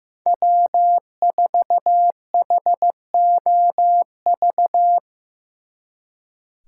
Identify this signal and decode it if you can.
Morse code, heard as a keyed tone, about 15 wpm, 700 Hz. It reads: W4HOV
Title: Speed Racing - USA 1x3 Call Signs 15wpm